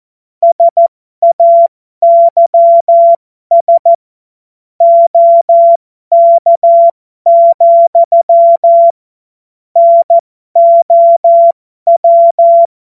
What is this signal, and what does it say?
Morse code, heard as a keyed tone, about 14 wpm, spaced as for 12 wpm, 680 Hz. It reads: SAYS OK, NOW
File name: Morsekurs / Witze